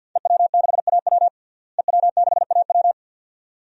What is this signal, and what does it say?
Morse code, heard as a keyed tone, about 50 wpm, 700 Hz. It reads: EW6AW EW6AW
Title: DXCC-100 Call Signs - 5 Characters - Repeated 2x - 50wpm